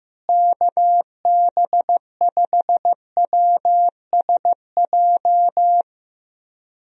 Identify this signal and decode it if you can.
Morse code, heard as a keyed tone, about 15 wpm, 700 Hz. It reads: KB5WSJ
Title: USA Call Signs - 6 Characters - Repeated 3x - 15wpm